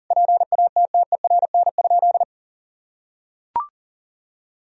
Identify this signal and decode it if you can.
Morse code, heard as a keyed tone, about 40 wpm, 700 Hz. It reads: PATTERN?
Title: Sentences from Top 1000 Words - Review Part 2 - 40wpm